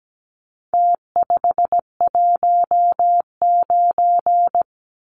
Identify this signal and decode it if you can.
Morse code, heard as a keyed tone, about 17 wpm, 700 Hz. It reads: T519
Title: Numbers - Contest Style - 4-Digit 17wpm